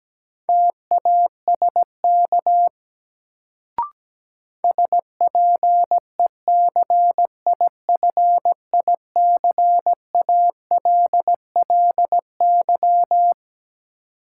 Morse code, heard as a keyed tone, about 17 wpm, 700 Hz. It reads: TASK SPECIFICALLY
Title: Sentences from Top 1100 Words - New Word Drill Alt - 17wpm